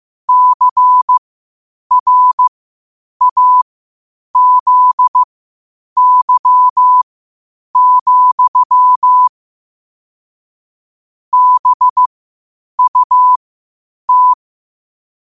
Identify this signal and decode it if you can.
Morse code, heard as a keyed tone, about 15 wpm, spaced as for 8 wpm, 1000 Hz. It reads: CRAZY, BUT
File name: qotdfiles / 8wpm